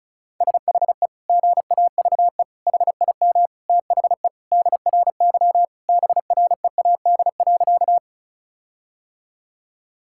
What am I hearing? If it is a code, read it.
Morse code, heard as a keyed tone, about 36 wpm, 700 Hz. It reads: SHE GAVE HIM THE DRY BREAD.
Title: Sentences from Top 1600 Words 35wpm